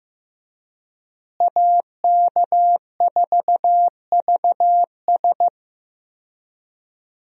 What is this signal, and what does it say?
Morse code, heard as a keyed tone, about 15 wpm, 700 Hz. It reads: AK4VS
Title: USA Call Signs - 5 Characters - Repeated 3x - 15wpm